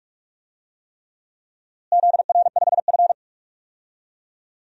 Morse code, heard as a keyed tone, about 47 wpm, 700 Hz. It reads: ZR5F
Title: DXCC-100 Call Signs - 4 Characters - Repeated 3x - 45wpm